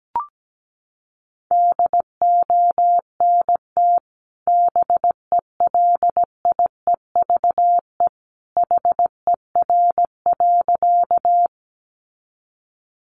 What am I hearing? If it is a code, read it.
Morse code, heard as a keyed tone, about 17 wpm, 700 Hz. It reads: DONT BELIEVE HER.
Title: Sentences from Top 1000 Words 17wpm